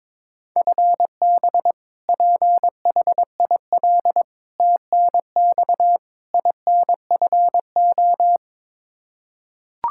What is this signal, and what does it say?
Morse code, heard as a keyed tone, about 22 wpm, 700 Hz. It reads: FB PHIL TNX INFO
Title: Practice QSO - Exchange 3 - More Info - 22wpm